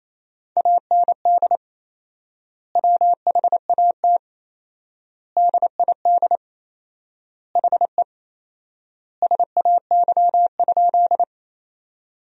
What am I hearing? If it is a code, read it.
Morse code, heard as a keyed tone, about 28 wpm, 700 Hz. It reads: AND WHAT DID HE SAY?
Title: 4x Word-Spacing - Sentences from Top 200 Words - 28wpm